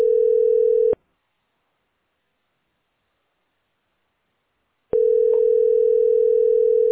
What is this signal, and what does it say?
Ringback tone (North America), 440 + 480 Hz.